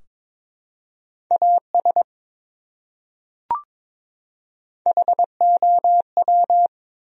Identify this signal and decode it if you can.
Morse code, heard as a keyed tone, about 21 wpm, 700 Hz. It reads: AS HOW
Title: Top 100 Words 22wpm